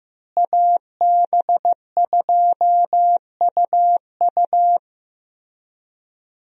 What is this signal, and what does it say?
Morse code, heard as a keyed tone, about 16 wpm, 700 Hz. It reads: AB2UU
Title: USA Call Signs - 5 Characters - Repeated 3x - 15wpm